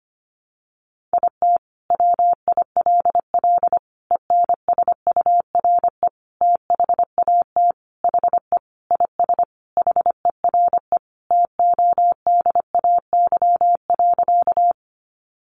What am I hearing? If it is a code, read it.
Morse code, heard as a keyed tone, about 25 wpm, 700 Hz. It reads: IT WILL ENSURE THAT HE IS HERE TODAY.
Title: Sentences from Top 1100 Words 25wpm